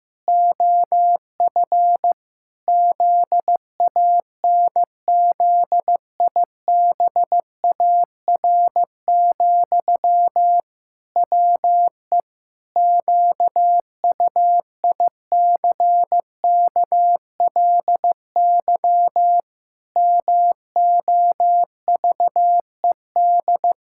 Morse code, heard as a keyed tone, about 15 wpm, 700 Hz. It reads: OF ZANZIBAR, WE QUICKLY MOVED
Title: Pangram Challenge v2 - 15wpm